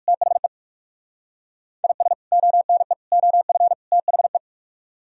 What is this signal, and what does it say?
Morse code, heard as a keyed tone, about 56 wpm, 700 Hz. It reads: THE IS ONE OF THE
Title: Speed-Racing - Sets of 4 Words 45wpm